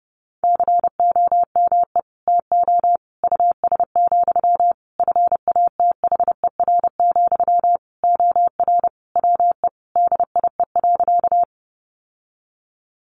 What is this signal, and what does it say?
Morse code, heard as a keyed tone, about 28 wpm, 700 Hz. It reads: COME TO US, FATHER, OR WE DIE.
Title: Sentences from Top 1300 Words 30wpm